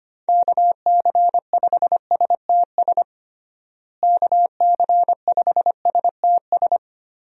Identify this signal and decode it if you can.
Morse code, heard as a keyed tone, about 25 wpm, 700 Hz. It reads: KC5STS KC5STS
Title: USA Call Signs - 6 Characters - Repeated 3x - 25wpm